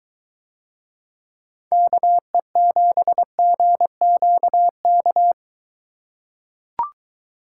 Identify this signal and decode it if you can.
Morse code, heard as a keyed tone, about 23 wpm, 700 Hz. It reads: KE7GQK